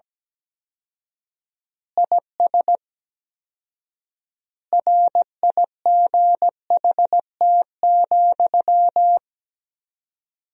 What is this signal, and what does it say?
Morse code, heard as a keyed tone, about 17 wpm, 700 Hz. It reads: IS RIGHT,
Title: 4x Word-Spacing - Sentences from Top 200 Words - 17wpm